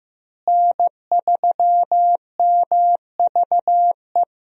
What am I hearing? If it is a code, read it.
Morse code, heard as a keyed tone, about 15 wpm, 700 Hz. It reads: N3MVE
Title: USA Call Signs - 5 Characters - Repeated 3x - 15wpm